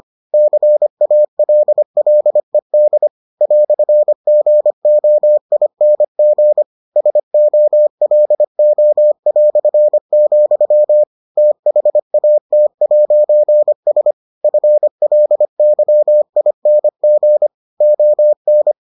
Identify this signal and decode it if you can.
Morse code, heard as a keyed tone, about 25 wpm, 600 Hz. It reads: CALLED "GOING SOLO", THAT'S FLYING ON